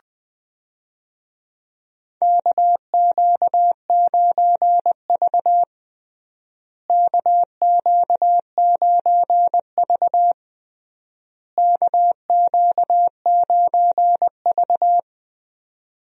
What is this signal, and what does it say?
Morse code, heard as a keyed tone, about 21 wpm, 700 Hz. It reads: KQ9V KQ9V KQ9V
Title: USA Call Signs - 4 Characters - Repeated 3x - 20wpm